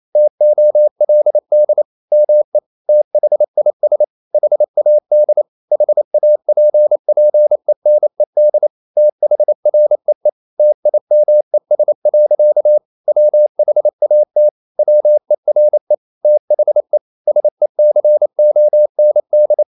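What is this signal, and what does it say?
Morse code, heard as a keyed tone, about 28 wpm, 600 Hz. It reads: TOLD ME THIS HAD HAPPENED THREE TIMES. WHAT WERE THE SECOND
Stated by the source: Morse code practice at 28 wpm, text of Ken Follett's novel Never in English, from CW med Gnister.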